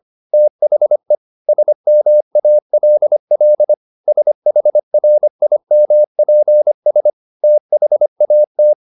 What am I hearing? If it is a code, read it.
Morse code, heard as a keyed tone, about 25 wpm, 600 Hz. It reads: THE SMALL SHRIMPS THAT